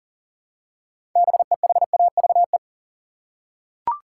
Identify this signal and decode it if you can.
Morse code, heard as a keyed tone, about 37 wpm, 700 Hz. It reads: BEHAVE
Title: Six Letter Words - 40wpm